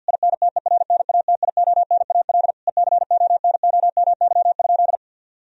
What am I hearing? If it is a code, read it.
Morse code, heard as a keyed tone, about 50 wpm, 700 Hz. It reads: INTERNATIONAL ECONOMY?
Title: Sentences from Top 1000 Words - Review Part 4 - 50wpm